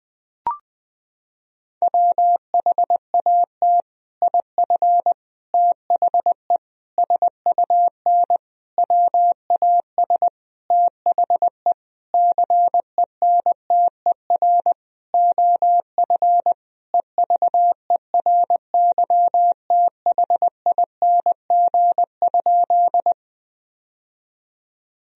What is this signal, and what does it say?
Morse code, heard as a keyed tone, about 20 wpm, 700 Hz. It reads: WHAT IF THE SUN WAS THE CENTER OF EVERYTHING?
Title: Sentences from Top 1000 Words - Review Part 2 - 20wpm